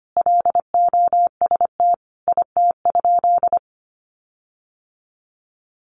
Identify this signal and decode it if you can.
Morse code, heard as a keyed tone, about 25 wpm, 700 Hz. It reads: LOST IT?
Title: Sentences from Top 1400 Words 25wpm